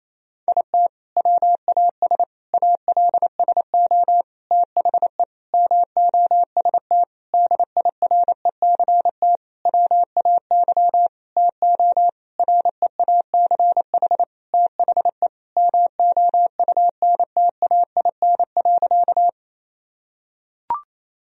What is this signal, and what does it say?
Morse code, heard as a keyed tone, about 28 wpm, 700 Hz. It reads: IT WAS ALSO THE MOST DIRECT WAY TO REACH THE MOUNTAIN.
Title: Sentences from Top 1000 Words - Review Part 2 - 28wpm